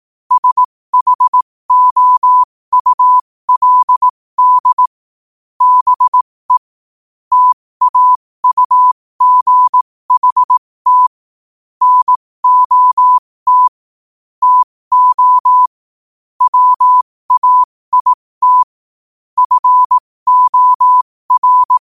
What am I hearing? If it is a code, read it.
Morse code, heard as a keyed tone, about 18 wpm, spaced as for 15 wpm, 1000 Hz. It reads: SHOULD BE TAUGHT NOT TO WAIT FOR
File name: qotdfiles / 15wpm